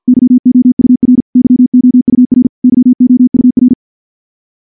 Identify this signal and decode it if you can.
Morse code, heard as a keyed tone, about 48 wpm, 250 Hz. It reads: YOUR YOUR YOUR
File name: morse / morse-training-50WPM